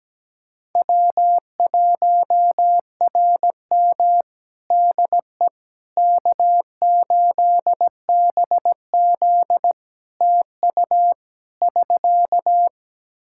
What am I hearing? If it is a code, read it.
Morse code, heard as a keyed tone, about 17 wpm, 700 Hz. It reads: W1RM DE K8BZ TU <SK>
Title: Practice QSO - Exchange 4 - Ending - 17wpm